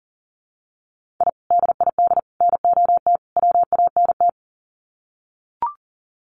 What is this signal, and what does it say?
Morse code, heard as a keyed tone, about 40 wpm, 700 Hz. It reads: I DID NOT WANT
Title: Sets of 4 Words 40wpm